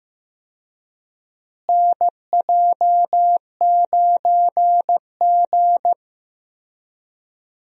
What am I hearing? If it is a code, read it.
Morse code, heard as a keyed tone, about 15 wpm, 700 Hz. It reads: NJ9G
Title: USA Call Signs - 4 Characters - Repeated 2x - 15wpm